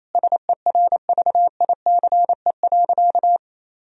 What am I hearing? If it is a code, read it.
Morse code, heard as a keyed tone, about 28 wpm, 700 Hz. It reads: SERVICE.
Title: Sentences from Top 1000 Words - Review Part 4 - 28wpm